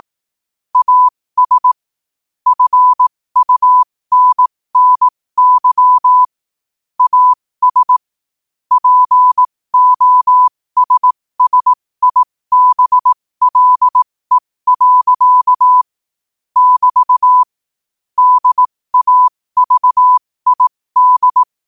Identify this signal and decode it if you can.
Morse code, heard as a keyed tone, about 18 wpm, spaced as for 15 wpm, 1000 Hz. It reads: AS FUNNY AS POSSIBLE. <BT> DAVID
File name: qotdfiles / 15wpm